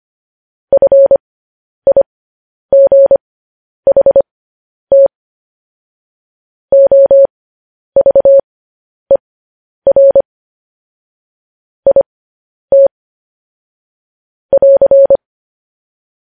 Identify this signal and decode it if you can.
Morse code, heard as a keyed tone, about 25 wpm, spaced as for 10 wpm, 550 Hz. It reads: FIGHT OVER IT <AR>
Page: news-headlines-in-morse-code-at-10-wpm